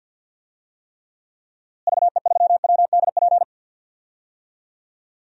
Morse code, heard as a keyed tone, about 51 wpm, 700 Hz. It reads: VE3WDP
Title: DXCC-100 Call Signs - 6 Characters - Repeated 3x - 50wpm